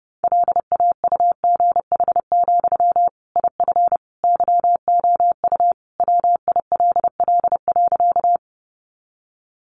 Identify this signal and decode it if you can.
Morse code, heard as a keyed tone, about 30 wpm, 700 Hz. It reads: LAUGH, IF YOU WILL.
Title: Sentences from Top 500 Words 30wpm